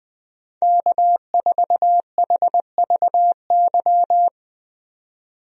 Morse code, heard as a keyed tone, about 20 wpm, 700 Hz. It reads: K4HVY
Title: USA Call Signs - 5 Characters - 20wpm